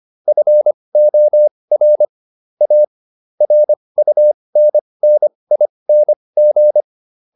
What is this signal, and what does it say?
Morse code, heard as a keyed tone, about 25 wpm, spaced as for 20 wpm, 600 Hz. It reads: FOR A RUNNING